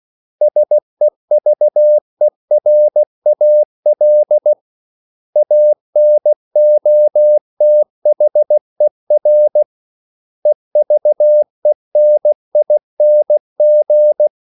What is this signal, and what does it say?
Morse code, heard as a keyed tone, about 16 wpm, 600 Hz. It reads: SEVERAL ANOTHER EVENING
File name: CW / Practice / 500MostCommonEnglishWords16